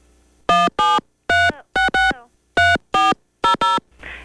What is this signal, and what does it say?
DTMF, heard as telephone touch tones, tones of about 190 ms. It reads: 173663400